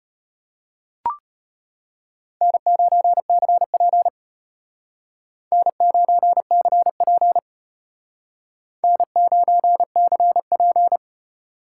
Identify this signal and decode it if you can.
Morse code, heard as a keyed tone, about 32 wpm, 700 Hz. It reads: N9CP N9CP N9CP